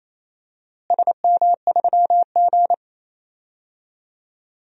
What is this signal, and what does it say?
Morse code, heard as a keyed tone, about 28 wpm, 700 Hz. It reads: SM3G
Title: DXCC-100 Call Signs - 4 Characters - Repeated 2x - 28wpm